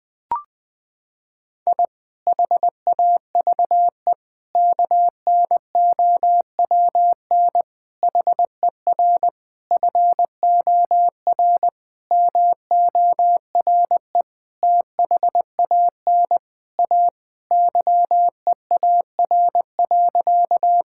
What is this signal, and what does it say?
Morse code, heard as a keyed tone, about 21 wpm, 700 Hz. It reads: I HAVE KNOWN HER FOR MORE THAN A YEAR.
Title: Sentences with 5-Word Phrases - 20wpm